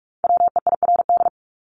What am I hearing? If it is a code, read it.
Morse code, heard as a keyed tone, about 44 wpm, 700 Hz. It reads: WEIRD
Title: Top 1251-1500 Words 45wpm